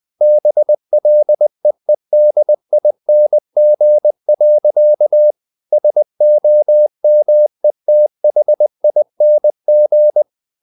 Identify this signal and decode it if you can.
Morse code, heard as a keyed tone, about 20 wpm, 600 Hz. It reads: BLEEDING. SOMETHING
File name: the old man and the sea / the old man and the sea part 1 20wpm